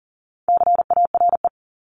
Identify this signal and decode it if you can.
Morse code, heard as a keyed tone, about 40 wpm, 700 Hz. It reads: CARE
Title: Top 500 Words 40wpm